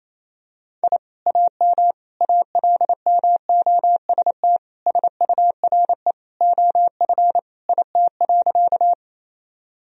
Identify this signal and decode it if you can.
Morse code, heard as a keyed tone, about 28 wpm, 700 Hz. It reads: I AM ALMOST SURE OF IT.
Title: Sentences from Top 1000 Words - Review Part 4 - 28wpm